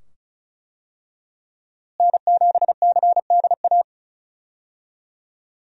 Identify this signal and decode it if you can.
Morse code, heard as a keyed tone, about 36 wpm, 700 Hz. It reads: N7CDA
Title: USA Call Signs - 5 Characters - Repeated 2x - 35wpm